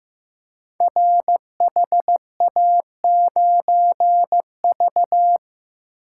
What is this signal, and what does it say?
Morse code, heard as a keyed tone, about 15 wpm, 700 Hz. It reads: RHA9V
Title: Code Groups - 5 Letter:Numbers - 15wpm